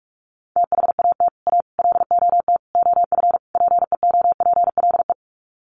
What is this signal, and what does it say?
Morse code, heard as a keyed tone, about 44 wpm, 700 Hz. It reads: THAT A LOT OF PEOPLE
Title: Sets of 5 Words 45wpm